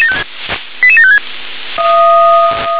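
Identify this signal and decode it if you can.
Five-tone selective call (ZVEI), heard as quick runs of five tones, tones of about 70 ms. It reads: 88756
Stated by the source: ZVEI five-tone selective call 88725 received on SDR.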